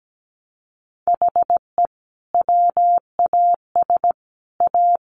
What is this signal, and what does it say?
Morse code, heard as a keyed tone, about 17 wpm, 700 Hz. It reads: HE WAS A
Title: Sentences from Top 1200 Words 17wpm